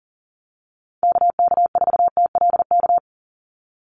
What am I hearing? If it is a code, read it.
Morse code, heard as a keyed tone, about 38 wpm, 700 Hz. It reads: KK4TLK
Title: USA Call Signs 40wpm